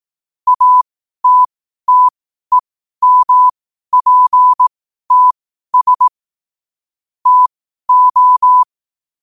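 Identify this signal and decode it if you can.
Morse code, heard as a keyed tone, about 18 wpm, spaced as for 12 wpm, 1000 Hz. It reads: ATTEMPTS TO
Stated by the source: Morse code quotes at 12 words per minute.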